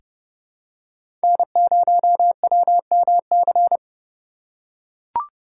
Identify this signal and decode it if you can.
Morse code, heard as a keyed tone, about 30 wpm, 700 Hz. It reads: N0WMC
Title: USA Call Signs - 5 Characters - 2x Character Spacing - 30wpm